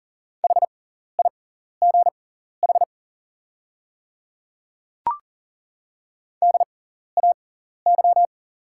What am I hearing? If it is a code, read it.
Morse code, heard as a keyed tone, about 40 wpm, spaced as for 14 wpm, 690 Hz. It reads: HIGH DAY
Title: 6x Character Spacing - Top 200 Words - 40wpm